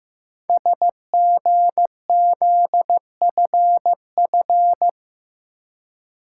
Morse code, heard as a keyed tone, about 15 wpm, 700 Hz. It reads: SGZFF
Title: Code Groups - 5 Letters - 15wpm